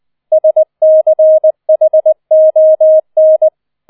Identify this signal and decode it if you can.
Morse code, heard as a keyed tone, about 20 wpm, 630 Hz. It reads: SCHON
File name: 20wpm